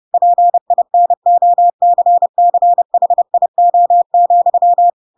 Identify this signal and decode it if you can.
Morse code, heard as a keyed tone, about 30 wpm, 700 Hz. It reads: PINOCCHIO,
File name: Pinocchio / Pinocchio ch7 30 30 ik1hge